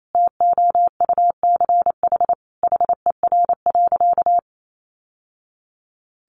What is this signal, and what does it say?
Morse code, heard as a keyed tone, about 28 wpm, 700 Hz. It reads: TOUCH HER.
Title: Sentences from Top 1200 Words 28wpm